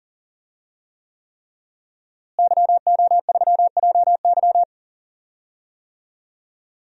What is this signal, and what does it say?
Morse code, heard as a keyed tone, about 40 wpm, 700 Hz. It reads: YO3JY
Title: DXCC-100 Call Signs - 5 Characters - Repeated 3x - 40wpm